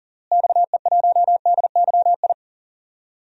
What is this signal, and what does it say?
Morse code, heard as a keyed tone, about 40 wpm, 700 Hz. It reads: XE1DYI
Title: DXCC-100 Call Signs - 6 Characters - Repeated 3x - 40wpm